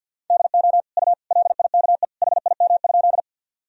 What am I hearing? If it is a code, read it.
Morse code, heard as a keyed tone, about 52 wpm, 700 Hz. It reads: DO U LIKE HIM?
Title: Sentences from Top 100 Words and Abbreviations 50wpm